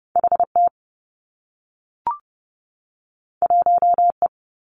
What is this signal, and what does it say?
Morse code, heard as a keyed tone, about 30 wpm, 700 Hz. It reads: H T 1 E
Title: Two Challenging Letter-Number Combinations 30wpm